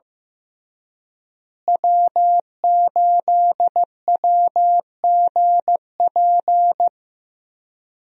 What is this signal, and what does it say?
Morse code, heard as a keyed tone, about 15 wpm, 700 Hz. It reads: W8WGP
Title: USA Call Signs - 5 Characters - Repeated 3x - 15wpm